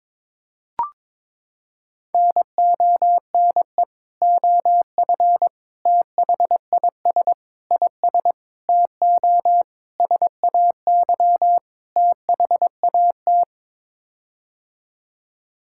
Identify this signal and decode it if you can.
Morse code, heard as a keyed tone, about 22 wpm, 700 Hz. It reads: NONE OF THIS IS TO SAY THAT
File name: practice / 22wpm / Sets of 7 Words - 22wpm